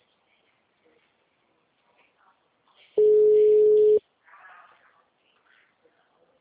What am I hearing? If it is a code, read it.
Ringback tone (Europe, 425 Hz).
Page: Sumare - SP